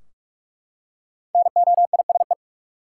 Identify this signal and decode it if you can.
Morse code, heard as a keyed tone, about 44 wpm, 700 Hz. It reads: NOISE
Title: Five Letter Words - 45wpm